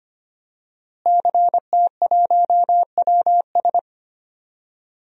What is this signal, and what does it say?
Morse code, heard as a keyed tone, about 25 wpm, 700 Hz. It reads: CT1WS
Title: DXCC-100 Call Signs - 5 Characters - Repeated 2x - 25wpm